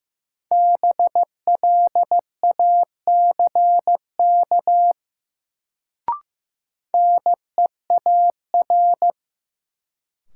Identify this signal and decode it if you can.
Morse code, heard as a keyed tone, about 15 wpm, 700 Hz. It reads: BLACK NEAR
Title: Sets of 5 Words Encore - New Word Drill - 15wpm